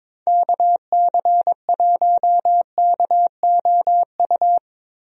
Morse code, heard as a keyed tone, about 22 wpm, 700 Hz. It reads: KC1KOU
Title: USA Call Signs - 6 Characters - Repeated 3x - 22wpm